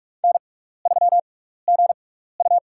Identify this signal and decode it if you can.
Morse code, heard as a keyed tone, about 44 wpm, spaced as for 15 wpm, 700 Hz. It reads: N3GU
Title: USA Call Signs - 4 Characters - 6x Character Spacing - 45wpm